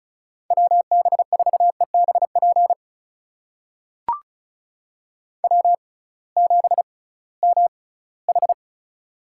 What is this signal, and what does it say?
Morse code, heard as a keyed tone, about 36 wpm, spaced as for 20 wpm, 700 Hz. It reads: WB4EBP W7MH
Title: USA Call Signs - 6x Character Spacing - 35wpm